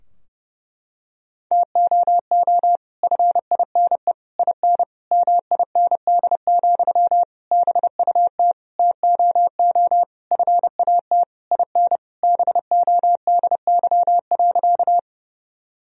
Morse code, heard as a keyed tone, about 32 wpm, 700 Hz. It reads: TOO FINE IN MIND, BUT TOO FAT IN BODY.
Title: Sentences from Top 1000 Words - Review Part 4 - 30wpm